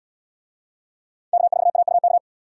Morse code, heard as a keyed tone, about 77 wpm, 700 Hz. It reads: K4AUK